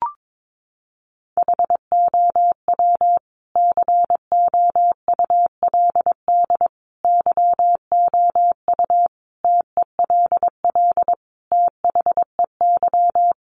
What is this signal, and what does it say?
Morse code, heard as a keyed tone, about 22 wpm, 700 Hz. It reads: HOW COULD YOU TELL THEY
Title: Sentences from Top 1000 Words 22wpm